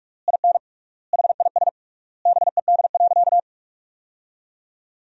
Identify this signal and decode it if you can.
Morse code, heard as a keyed tone, about 46 wpm, 700 Hz. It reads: IN HIS BED.
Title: 3x Word-Spacing - Sentences from Top 500 Words - 45wpm